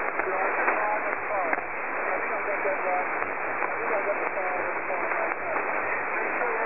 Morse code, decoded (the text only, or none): none